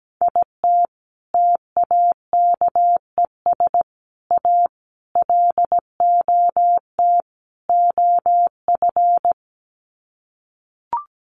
Morse code, heard as a keyed tone, about 17 wpm, 700 Hz. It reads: IT TAKES A LOT OF
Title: Sets of 5 Words Encore 17wpm